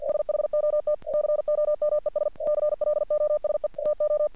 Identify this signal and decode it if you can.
Morse code, heard as a keyed tone, about 48 wpm, 600 Hz. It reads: SHOT COMES CLOSE TO
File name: archived / CRSnet